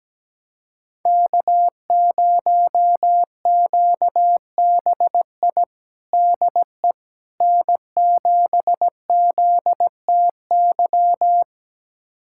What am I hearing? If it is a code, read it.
Morse code, heard as a keyed tone, about 17 wpm, 700 Hz. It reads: K0QBI DE N7ZTY
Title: USA Callsign DE Callsign - 5 Characters - 17wpm